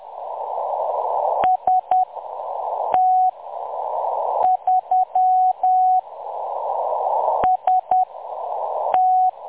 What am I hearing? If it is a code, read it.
Morse code, heard as a keyed tone, about 10 wpm, 740 Hz. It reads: S T 3 S T
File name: M23-ST3- -0858utc-10184khz